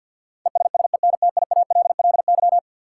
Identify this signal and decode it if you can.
Morse code, heard as a keyed tone, about 53 wpm, 700 Hz. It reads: ESSENTIALLY
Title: Sentences from Top 1300 Words - New Word Drill Alt - 50wpm